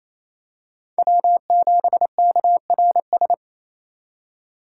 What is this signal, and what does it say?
Morse code, heard as a keyed tone, about 28 wpm, 700 Hz. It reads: W7KRS
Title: K1USN SST QSO Practice - 28wpm